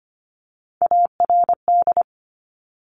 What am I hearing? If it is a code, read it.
Morse code, heard as a keyed tone, about 25 wpm, 700 Hz. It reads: ARD
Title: Top Three Letter Combinations in Words 25wpm